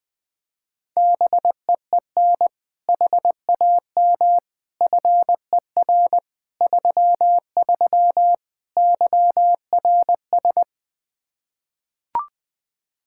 Morse code, heard as a keyed tone, about 20 wpm, 700 Hz. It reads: BEEN HAM FER 33 YRS
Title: Practice QSO - Exchange 3 - More Info - 20wpm